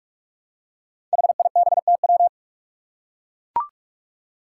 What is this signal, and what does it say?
Morse code, heard as a keyed tone, about 43 wpm, 700 Hz. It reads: HIBTW
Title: Five Challenging Letter-Number Combinations 45wpm